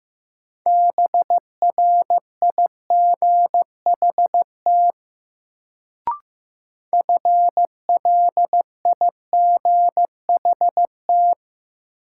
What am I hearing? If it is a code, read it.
Morse code, heard as a keyed tone, about 15 wpm, 700 Hz. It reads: BRIGHT FLIGHT
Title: Sentences from Top 1000 Words - New Word Drill - 15wpm